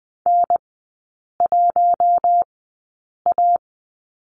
Morse code, heard as a keyed tone, about 20 wpm, spaced as for 8 wpm, 700 Hz. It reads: N1A
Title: 041 Lesson 8e - Number 1 - Three Characters 20wpm